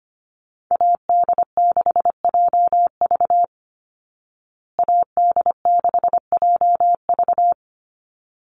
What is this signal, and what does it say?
Morse code, heard as a keyed tone, about 25 wpm, 700 Hz. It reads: AD6JV AD6JV
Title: Call Signs Repeated 3x - 25wpm